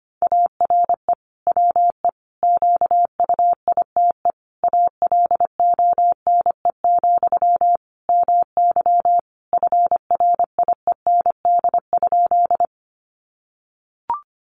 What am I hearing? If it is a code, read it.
Morse code, heard as a keyed tone, about 25 wpm, 700 Hz. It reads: ARE WE QUITE ALONE, MY FRIEND?
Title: Sentences from Top 1000 Words 25wpm